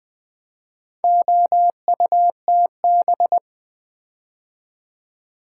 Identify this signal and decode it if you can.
Morse code, heard as a keyed tone, about 20 wpm, 700 Hz. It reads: OUTB
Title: Build A Word - 4 Letters At A Time - 8-Letter Words - 20wpm